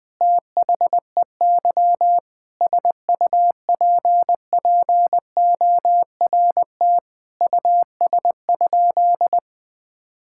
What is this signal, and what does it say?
Morse code, heard as a keyed tone, about 20 wpm, 700 Hz. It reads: THEY SUPPORT US?
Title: Sentences from Top 1000 Words - Review Part 5 - 20wpm